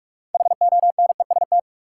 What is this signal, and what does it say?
Morse code, heard as a keyed tone, about 46 wpm, 700 Hz. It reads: HONEST
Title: Sets of 5 Words Encore - New Word Drill - 45wpm